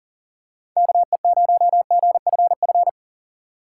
Morse code, heard as a keyed tone, about 40 wpm, 700 Hz. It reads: KE0GFF
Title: USA Call Signs - 6 Characters - 6x Character Spacing - 40wpm